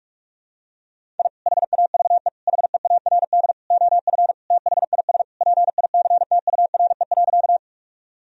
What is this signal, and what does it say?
Morse code, heard as a keyed tone, about 41 wpm, 700 Hz. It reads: I HAVE HEARD OF THIS PICTURE.
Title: Sentences from Top 1000 Words - Review Part 1 - 45wpm